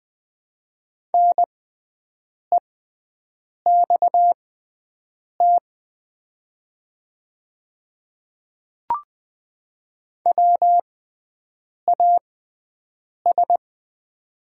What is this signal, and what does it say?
Morse code, heard as a keyed tone, about 20 wpm, spaced as for 7 wpm, 690 Hz. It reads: NEXT WAS
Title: 6x Character Spacing - Top 300 Words - 20wpm